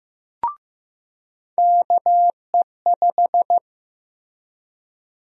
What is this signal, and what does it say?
Morse code, heard as a keyed tone, about 15 wpm, 700 Hz. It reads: KE5
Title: USA Call Sign Prefixes - Repeated 2x - 15wpm